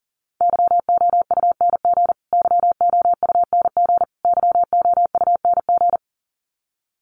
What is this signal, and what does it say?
Morse code, heard as a keyed tone, about 40 wpm, 700 Hz. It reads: YOUNG YOUNG YOUNG
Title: Top 500 Words Repeated 40wpm